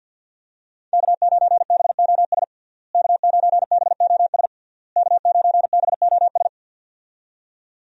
Morse code, heard as a keyed tone, about 46 wpm, 700 Hz. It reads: K9BOS K9BOS K9BOS
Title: USA Call Signs - 5 Characters - Repeated 3x - 50wpm